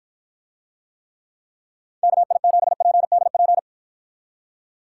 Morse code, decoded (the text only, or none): KI7PDP